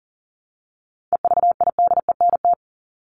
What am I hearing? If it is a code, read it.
Morse code, heard as a keyed tone, about 40 wpm, 700 Hz. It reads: EVIDENT
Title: LY Suffix - Top 100 Words with Base Words 40wpm